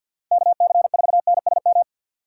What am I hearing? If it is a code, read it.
Morse code, heard as a keyed tone, about 45 wpm, 690 Hz. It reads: KC4NSM
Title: USA Call Signs - 6 Characters - 3x Character Spacing - 50wpm